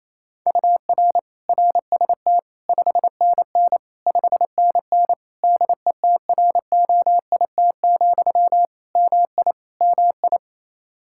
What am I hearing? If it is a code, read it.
Morse code, heard as a keyed tone, about 28 wpm, 700 Hz. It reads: UR RST 5NN 5NN DETROIT, MI MI
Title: Practice QSO - Exchange 1 - Beginning - 28wpm